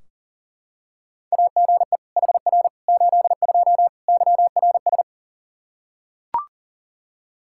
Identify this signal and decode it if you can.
Morse code, heard as a keyed tone, about 37 wpm, 700 Hz. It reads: AGE HR 82 YRS
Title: K1USN SST QSO Practice 40wpm